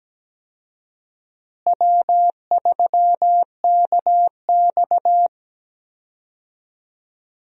Morse code, heard as a keyed tone, about 17 wpm, 700 Hz. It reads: W3KX
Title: USA Call Signs - 4 Characters - Repeated 2x - 17wpm